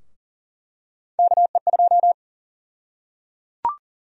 Morse code, heard as a keyed tone, about 37 wpm, 700 Hz. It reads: KE2
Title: USA Call Sign Prefixes - Repeated 2x - 40wpm